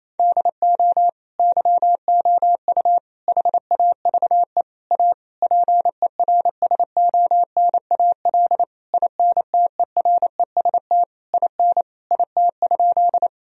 Morse code, heard as a keyed tone, about 28 wpm, 700 Hz. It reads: DO YOU HAVE A PERSONAL INTEREST IN IT?
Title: Sentences from Top 1000 Words - Review Part 4 - 28wpm